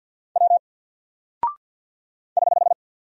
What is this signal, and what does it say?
Morse code, heard as a keyed tone, about 48 wpm, 700 Hz. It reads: W <HH>
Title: Single Character (Letter:Number:Punctuation:Prosign) Alt 50wpm